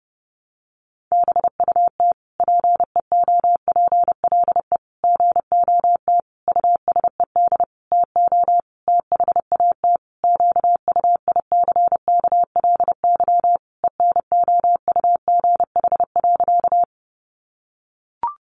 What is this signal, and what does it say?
Morse code, heard as a keyed tone, about 30 wpm, 700 Hz. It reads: BUT PEOPLE GOT USED TO THAT QUICKLY ENOUGH.
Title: Sentences from Top 700 Words 30wpm